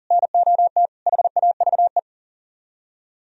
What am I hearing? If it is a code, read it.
Morse code, heard as a keyed tone, about 50 wpm, 700 Hz. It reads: NOT HAVE
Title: Speed-Racing - Sets of 4 Words 40wpm